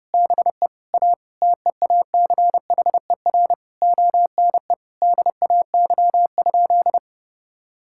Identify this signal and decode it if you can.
Morse code, heard as a keyed tone, about 30 wpm, 700 Hz. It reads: BE A TEACHER ONE DAY?
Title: Sentences from Top 1000 Words - Review Part 6 - 30wpm